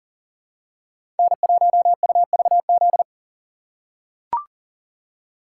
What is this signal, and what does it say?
Morse code, heard as a keyed tone, about 40 wpm, 700 Hz. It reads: N1UVZ
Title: USA Call Signs - 5 Characters - 2x Character Spacing - 40wpm